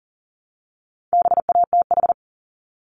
Morse code, heard as a keyed tone, about 39 wpm, 700 Hz. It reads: BATH
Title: Multi-letter Phonemes Repeated 3x 40wpm